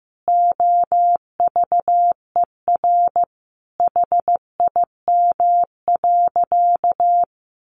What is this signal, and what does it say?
Morse code, heard as a keyed tone, about 15 wpm, 700 Hz. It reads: OVER HIM.
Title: Sentences from Top 1100 Words 15wpm